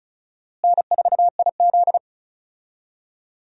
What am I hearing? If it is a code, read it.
Morse code, heard as a keyed tone, about 36 wpm, 700 Hz. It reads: N4IZ
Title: USA Call Signs - 4 Characters - Repeated 2x - 35wpm